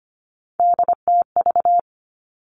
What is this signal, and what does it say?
Morse code, heard as a keyed tone, about 25 wpm, 700 Hz. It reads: DTV
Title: Three Challenging Letter-Number Combinations 25wpm